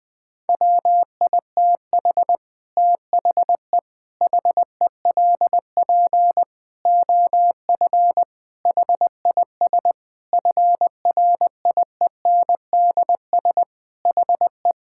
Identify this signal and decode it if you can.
Morse code, heard as a keyed tone, about 20 wpm, 700 Hz. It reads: WITH THE HELP OF HIS FRIENDS HE
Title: Sentences with 4-Word Phrases - 20wpm